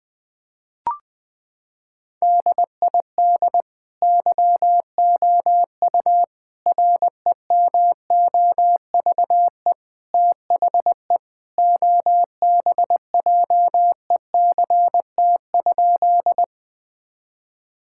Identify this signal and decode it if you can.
Morse code, heard as a keyed tone, about 20 wpm, 700 Hz. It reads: DID YOU REMOVE THE OBJECT?
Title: Sentences from Top 1000 Words - Review Part 5 - 20wpm